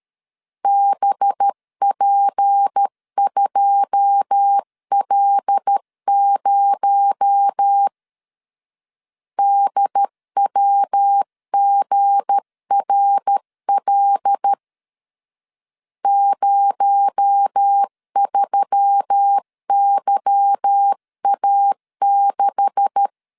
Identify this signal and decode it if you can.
Morse code, heard as a keyed tone, about 13 wpm, 800 Hz. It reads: BP2L0 DWGRL 03YA6
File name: LEZIONI / 9 286 12wpm 2305